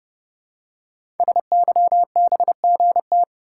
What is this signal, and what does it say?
Morse code, heard as a keyed tone, about 28 wpm, 700 Hz. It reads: SYBGT
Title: Code Groups - 5 Letters - 30wpm